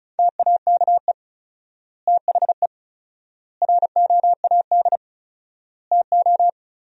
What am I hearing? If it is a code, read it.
Morse code, heard as a keyed tone, about 36 wpm, 700 Hz. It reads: TAKE THE ROAD TO
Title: 4x Word-Spacing - Sentences from Top 400 Words - 35wpm